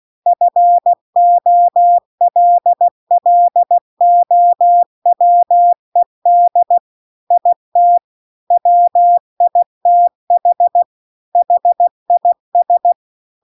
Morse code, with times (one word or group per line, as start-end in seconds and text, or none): 0.3-6.8 FOLLOWED
7.3-8.0 IT
8.5-10.8 WITH
11.4-12.9 HIS